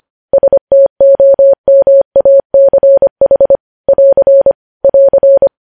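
Morse code, heard as a keyed tone, about 25 wpm, 550 Hz. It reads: STOMACH <AR> <AR>
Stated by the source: Morse code news headlines sent at 25 wpm.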